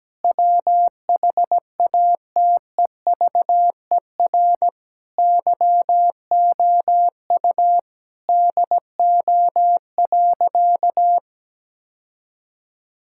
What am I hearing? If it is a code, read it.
Morse code, heard as a keyed tone, about 17 wpm, 700 Hz. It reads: WHATEVER YOU DO.
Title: Sentences from Top 1600 Words 17wpm